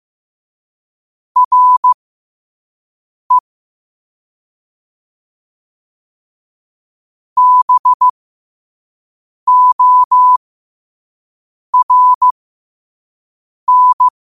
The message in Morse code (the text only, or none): REBORN